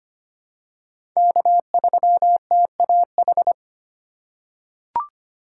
Morse code, heard as a keyed tone, about 25 wpm, 700 Hz. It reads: K3TAH
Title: USA Call Signs - 5 Characters - 4x Character Spacing - 25wpm